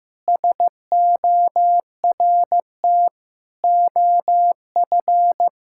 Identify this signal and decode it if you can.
Morse code, heard as a keyed tone, about 15 wpm, 700 Hz. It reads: SORT OF
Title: Sentences from Top 1000 Words - Review Part 5 - 15wpm